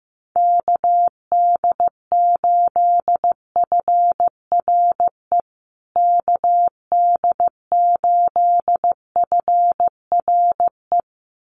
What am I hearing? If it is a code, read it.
Morse code, heard as a keyed tone, about 15 wpm, 700 Hz. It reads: KD8FRE KD8FRE
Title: USA Call Signs Repeated 2x 15wpm